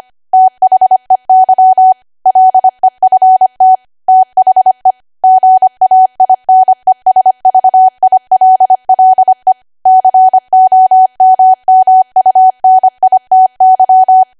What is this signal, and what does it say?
Morse code, heard as a keyed tone, about 25 wpm, 750 Hz. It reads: THEY LEFT THE GAINESVILLE COMMUNITY
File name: Archive / 25 WPM / 25WPM